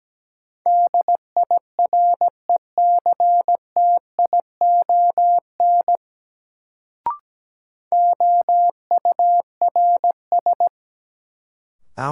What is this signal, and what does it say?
Morse code, heard as a keyed tone, about 17 wpm, 700 Hz. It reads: DIRECTION OURS
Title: Sentences from Top 1000 Words - Review Part 5 - New Word Drill - 17wpm